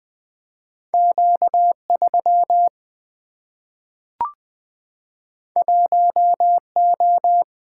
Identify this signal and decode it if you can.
Morse code, heard as a keyed tone, about 20 wpm, 700 Hz. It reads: Q3 1O
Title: Single-Letter Single-Number 20wpm